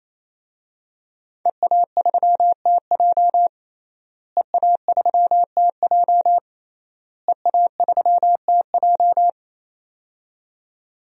Morse code, heard as a keyed tone, about 28 wpm, 700 Hz. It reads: EA3TJ EA3TJ EA3TJ
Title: DXCC-100 Call Signs - 5 Characters - Repeated 3x - 28wpm